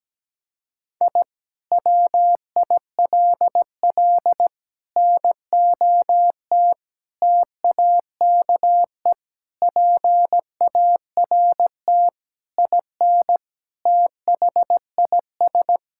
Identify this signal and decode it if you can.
Morse code, heard as a keyed tone, about 17 wpm, 700 Hz. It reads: I WILL NOT TAKE PART IN THIS
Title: Sentences from Top 1000 Words - Review Part 6 - 17wpm